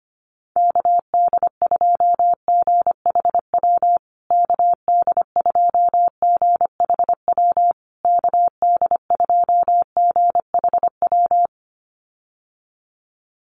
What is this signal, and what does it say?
Morse code, heard as a keyed tone, about 25 wpm, 700 Hz. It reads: KD2GHW KD2GHW KD2GHW
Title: USA Call Signs Repeated 3x 25wpm